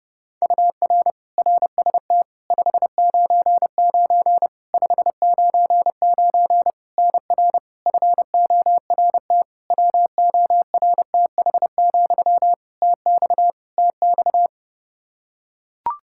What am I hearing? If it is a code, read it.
Morse code, heard as a keyed tone, about 30 wpm, 700 Hz. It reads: UR RST 599 599 NR FORT WORTH, TX TX
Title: Practice QSO - Exchange 1 - Beginning - 30wpm